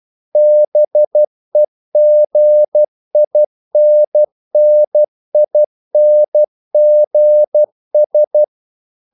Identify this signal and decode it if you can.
Morse code, heard as a keyed tone, about 12 wpm, 600 Hz. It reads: BEGINNINGS
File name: CW / Practice / If12